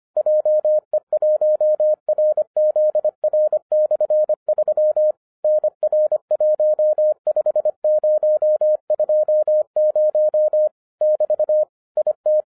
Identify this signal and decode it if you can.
Morse code, heard as a keyed tone, about 26 wpm, 610 Hz. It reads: JE1RZR/3 NR15020 <BT> IT
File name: 2011-10-mp3 / n04-JUST MOVED 25wpm